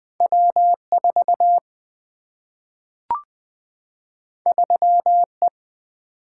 Morse code, heard as a keyed tone, about 20 wpm, 700 Hz. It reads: W4 3E
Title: Single-Letter Single-Number 20wpm